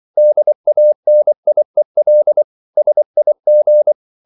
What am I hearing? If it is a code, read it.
Morse code, heard as a keyed tone, about 24 wpm, 600 Hz. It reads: DANIEL SIG
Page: Ferie 24 wpm